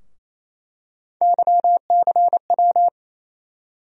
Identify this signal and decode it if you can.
Morse code, heard as a keyed tone, about 28 wpm, 700 Hz. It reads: YCW
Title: USA Call Sign Postfixes - 3 Letters 28wpm